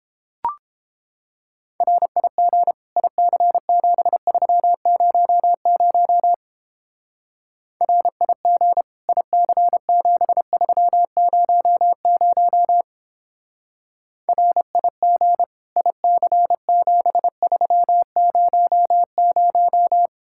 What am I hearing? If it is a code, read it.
Morse code, heard as a keyed tone, about 29 wpm, 700 Hz. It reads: RIG IC7300 RIG IC7300 RIG IC7300